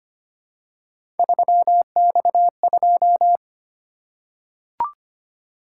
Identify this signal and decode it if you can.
Morse code, heard as a keyed tone, about 25 wpm, 700 Hz. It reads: 3X2
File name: Three Letter-Number Combinations - 25wpm